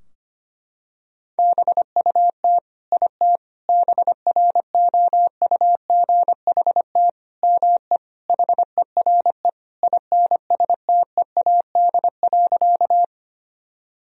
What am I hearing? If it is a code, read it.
Morse code, heard as a keyed tone, about 25 wpm, 700 Hz. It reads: BUT IT BROUGHT ME HERE INSTEAD.
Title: Sentences from Top 1000 Words - Review Part 4 - 25wpm